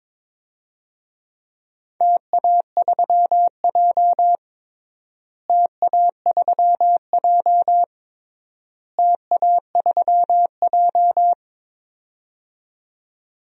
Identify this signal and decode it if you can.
Morse code, heard as a keyed tone, about 22 wpm, 700 Hz. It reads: TA3J TA3J TA3J
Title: DXCC-100 Call Signs - 4 Characters - Repeated 3x - 22wpm